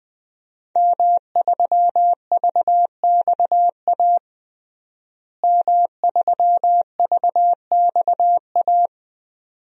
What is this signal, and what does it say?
Morse code, heard as a keyed tone, about 20 wpm, 700 Hz. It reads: M3VXA M3VXA
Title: DXCC-100 Call Signs - 5 Characters - Repeated 2x - 20wpm